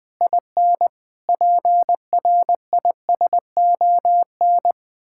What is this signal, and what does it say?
Morse code, heard as a keyed tone, about 20 wpm, 700 Hz. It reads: IN PRISON
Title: Sets of 8 Words - 20wpm